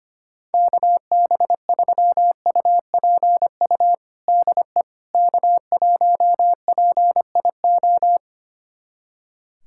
Morse code, heard as a keyed tone, about 25 wpm, 700 Hz. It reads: KB3UPU DE K1PIO
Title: USA Callsign DE Callsign 25wpm